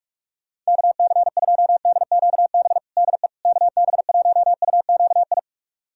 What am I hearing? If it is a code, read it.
Morse code, heard as a keyed tone, about 46 wpm, 700 Hz. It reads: KC2DQB DE KB1UQI
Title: USA Callsign DE Callsign 45wpm